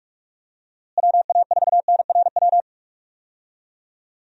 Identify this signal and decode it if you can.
Morse code, heard as a keyed tone, about 45 wpm, 700 Hz. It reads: WA4NRW
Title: USA Call Signs - 6 Characters - Repeated 3x - 45wpm